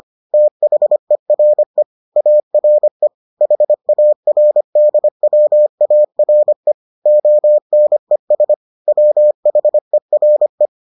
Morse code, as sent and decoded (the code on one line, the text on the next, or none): - .... .-. / .- .-. / .... .- .-. -.. .-- .- .-. / --- -. ... / .-- .... .-. .
THERE ARE HARDWARE ONES WHERE